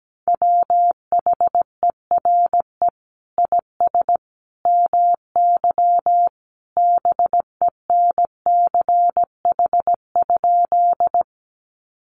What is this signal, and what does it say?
Morse code, heard as a keyed tone, about 17 wpm, 700 Hz. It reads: WHERE IS MY BENCH?
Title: Sentences from Top 1400 Words 17wpm